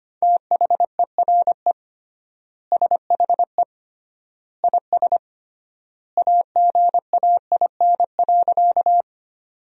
Morse code, heard as a keyed tone, about 25 wpm, 700 Hz. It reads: THERE SHE IS AGAIN.
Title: 3x Word-Spacing - Sentences from Top 200 Words - 25wpm